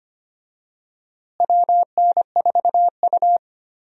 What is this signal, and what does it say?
Morse code, heard as a keyed tone, about 25 wpm, 700 Hz. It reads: WN4U
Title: USA Call Signs - 4 Characters - Repeated 2x - 25wpm